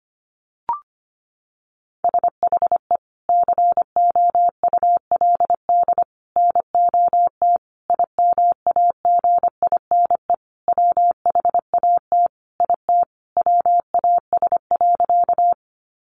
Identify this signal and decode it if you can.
Morse code, heard as a keyed tone, about 25 wpm, 700 Hz. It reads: SHE COULD NOT IMAGINE WHAT IT WAS.
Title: Sentences from Top 1400 Words 25wpm